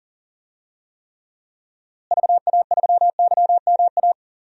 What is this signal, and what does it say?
Morse code, heard as a keyed tone, about 38 wpm, 700 Hz. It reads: VA3YMA
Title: DXCC-100 Call Signs - 6 Characters - Repeated 2x - 40wpm